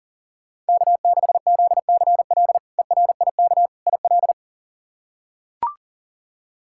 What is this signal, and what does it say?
Morse code, heard as a keyed tone, about 37 wpm, 700 Hz. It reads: K6ZCL ERIK IL
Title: K1USN SST QSO Practice - 40wpm